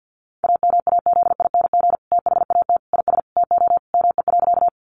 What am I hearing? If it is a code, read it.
Morse code, heard as a keyed tone, about 50 wpm, 700 Hz. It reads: AMAZING THAT IS TO ME.
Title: Sentences from Top 1500 Words 50wpm